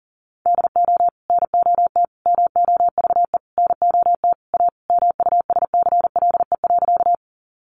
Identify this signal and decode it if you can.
Morse code, heard as a keyed tone, about 40 wpm, 700 Hz. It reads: DO NOT MOVE NOT A MUSCLE.
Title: Sentences from Top 1300 Words 40wpm